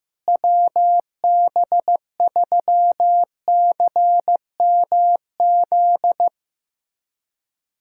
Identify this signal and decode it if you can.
Morse code, heard as a keyed tone, about 15 wpm, 700 Hz. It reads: WB3CMZ
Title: USA Callsign DE Callsign 15wpm